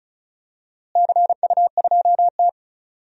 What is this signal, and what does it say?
Morse code, heard as a keyed tone, about 36 wpm, 700 Hz. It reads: CU2T
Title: DXCC-100 Call Signs - 4 Characters - Repeated 3x - 35wpm